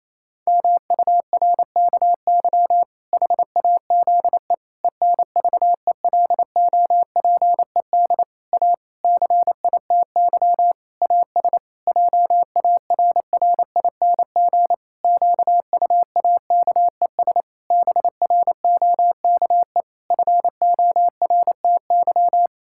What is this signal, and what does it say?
Morse code, heard as a keyed tone, about 27 wpm, 700 Hz. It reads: MURKY HAZE ENVELOPED A CITY AS JARRING QUAKES BROKE FORTY
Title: Pangram Challenge v2 - 28wpm